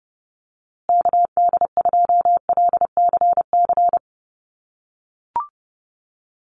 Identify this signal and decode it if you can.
Morse code, heard as a keyed tone, about 29 wpm, 700 Hz. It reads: KD2LCC
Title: USA Call Signs Repeated 2x 30wpm